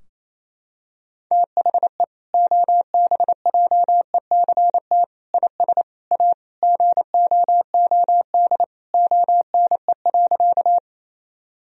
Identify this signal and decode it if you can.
Morse code, heard as a keyed tone, about 28 wpm, 700 Hz. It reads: THE OBJECT IS A GOOD ONE.
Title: Sentences from Top 1000 Words - Review Part 3 - 28wpm